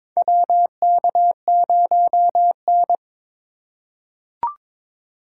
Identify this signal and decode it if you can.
Morse code, heard as a keyed tone, about 22 wpm, 700 Hz. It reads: WK0N
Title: USA Call Signs - 4 Characters - 3x Character Spacing - 22wpm